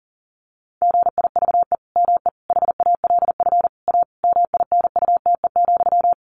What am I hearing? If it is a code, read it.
Morse code, heard as a keyed tone, about 40 wpm, 700 Hz. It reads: GIVE ME HALF A MINUTE,
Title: Sentences from Top 1500 Words 40wpm